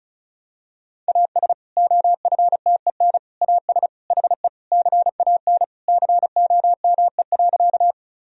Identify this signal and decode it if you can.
Morse code, heard as a keyed tone, about 36 wpm, 700 Hz. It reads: AS OFTEN AS HE CAN COME.
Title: Sentences from Top 1000 Words - Review Part 1 - 35wpm